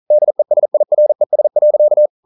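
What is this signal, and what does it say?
Morse code, heard as a keyed tone, about 42 wpm, 600 Hz. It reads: DESIRES.
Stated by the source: Morse code practice at 41 wpm, from CW med Gnister.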